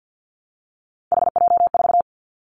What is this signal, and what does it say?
Morse code, heard as a keyed tone, about 50 wpm, 700 Hz. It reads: HJ4
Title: Callsign Prefixes for Top 100 DXCC - 50wpm